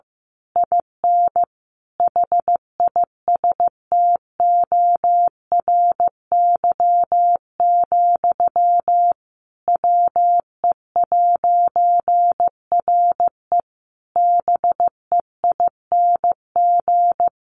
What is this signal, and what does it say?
Morse code, heard as a keyed tone, about 15 wpm, 700 Hz. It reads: IN HISTORY, WE'RE BEING
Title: A Sound Of Thunder 15wpm